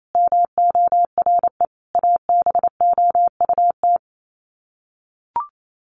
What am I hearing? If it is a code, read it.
Morse code, heard as a keyed tone, about 28 wpm, 700 Hz. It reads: MORE ABOUT
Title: Sets of 5 Words Encore2 28wpm